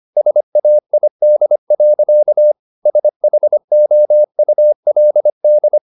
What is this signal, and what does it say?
Morse code, heard as a keyed tone, about 25 wpm, 600 Hz. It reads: SAID. SHOULD
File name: the old man and the sea part 1 25wpm